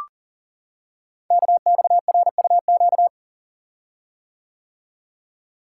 Morse code, heard as a keyed tone, about 39 wpm, 690 Hz. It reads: KXRUQ
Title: Code Groups - 5 Letters - 40wpm